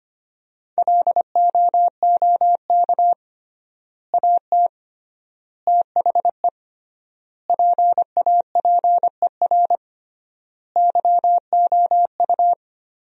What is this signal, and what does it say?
Morse code, heard as a keyed tone, about 25 wpm, 700 Hz. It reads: LOOK AT THE PAPER YOU
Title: 3x Word-Spacing - Sentences from Top 400 Words - 25wpm